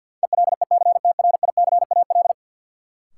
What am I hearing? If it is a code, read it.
Morse code, heard as a keyed tone, about 53 wpm, 700 Hz. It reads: ELECTRICAL
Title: Sentences from Top 1000 Words - Review Part 6 - New Word Drill - 50wpm